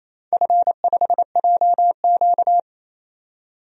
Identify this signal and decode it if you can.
Morse code, heard as a keyed tone, about 28 wpm, 700 Hz. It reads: F5JQ
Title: DXCC-100 Call Signs - 4 Characters - 28wpm